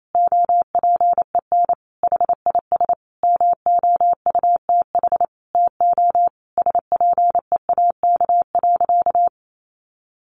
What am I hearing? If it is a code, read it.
Morse code, heard as a keyed tone, about 28 wpm, 700 Hz. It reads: OPEN HIS MOUTH TO SPEAK.
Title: Sentences from Top 1500 Words 28wpm